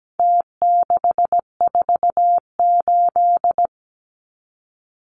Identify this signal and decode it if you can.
Morse code, heard as a keyed tone, about 17 wpm, 700 Hz. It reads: T648
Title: Numbers - Contest Style - 4-Digit 17wpm